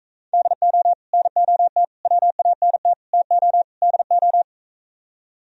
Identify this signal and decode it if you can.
Morse code, heard as a keyed tone, about 45 wpm, 700 Hz. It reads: DO NOT WANT TO DO